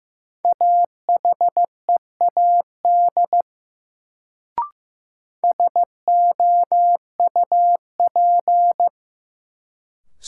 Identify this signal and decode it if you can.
Morse code, heard as a keyed tone, about 15 wpm, 700 Hz. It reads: AHEAD SOUP
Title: Sentences from Top 1000 Words - Review Part 6 - New Word Drill - 15wpm